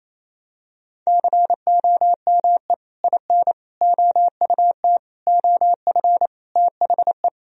Morse code, heard as a keyed tone, about 28 wpm, 700 Hz. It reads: COME IN OUT OF THE
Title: Sentences from Top 1000 Words - Review Part 1 - 28wpm